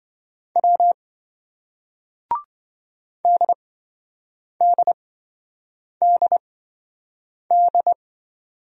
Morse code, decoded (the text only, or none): W DDDD